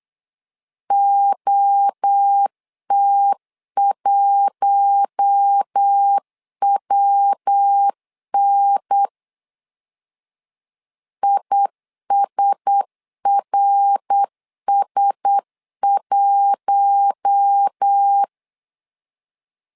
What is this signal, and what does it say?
Morse code, heard as a keyed tone, about 8 wpm, 800 Hz. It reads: OT1WN ISRS1
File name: LEZIONI / 4 Q1S 8wpm 2710